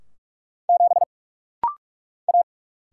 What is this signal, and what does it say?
Morse code, heard as a keyed tone, about 44 wpm, 700 Hz. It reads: 7 A
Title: Single Letter-Number - Rapid-Fire 45wpm